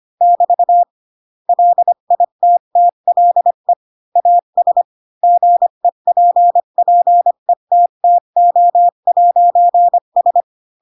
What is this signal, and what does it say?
Morse code, heard as a keyed tone, about 25 wpm, 700 Hz. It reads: <BT> LITTLE AS GEPPETTO'S